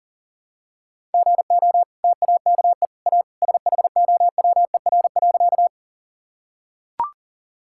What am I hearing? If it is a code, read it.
Morse code, heard as a keyed tone, about 37 wpm, 700 Hz. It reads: GO TAKE A SHOWER.
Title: Sentences from Top 1600 Words 40wpm